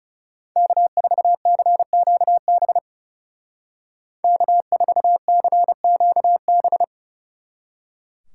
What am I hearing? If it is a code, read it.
Morse code, heard as a keyed tone, about 38 wpm, 700 Hz. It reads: K4CQB K4CQB